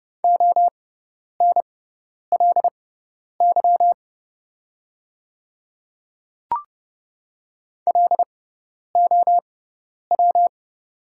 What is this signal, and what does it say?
Morse code, heard as a keyed tone, about 30 wpm, spaced as for 10 wpm, 700 Hz. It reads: ONLY LOW
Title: 6x Character Spacing - Top 200 Words - 30wpm